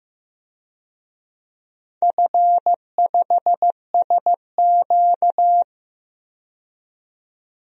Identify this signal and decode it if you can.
Morse code, heard as a keyed tone, about 15 wpm, 700 Hz. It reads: F5SQ
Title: DXCC-100 Call Signs - 4 Characters - 15wpm